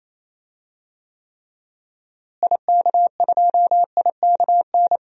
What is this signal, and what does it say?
Morse code, heard as a keyed tone, about 28 wpm, 700 Hz. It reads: IK2IKN
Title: DXCC-100 Call Signs - 6 Characters - Repeated 3x - 28wpm